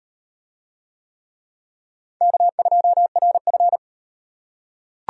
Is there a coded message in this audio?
Yes: Morse code from 2.2 to 3.8 s, heard as a keyed tone, about 35 wpm, 700 Hz. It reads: K2RF